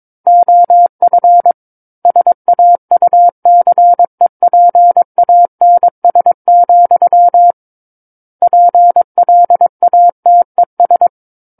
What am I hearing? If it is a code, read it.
Morse code, heard as a keyed tone, about 22 wpm, 700 Hz. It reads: OF SAUCEPANS, PLATES
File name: 06-Chapter-06 22 WPM